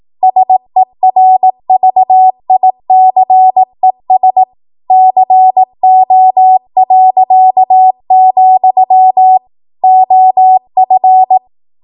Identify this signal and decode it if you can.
Morse code, heard as a keyed tone, about 18 wpm, 750 Hz. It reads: SERVICES CO., OF